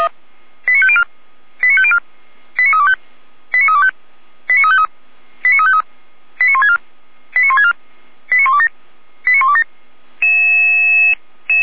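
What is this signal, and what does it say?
Five-tone selective call (ZVEI), heard as quick runs of five tones, tones of about 70 ms. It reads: 89504 89504 89325 89325 89353 89353 89164 89164 89127 89127
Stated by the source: ZVEI five-tone selective call 89126 received by SDR.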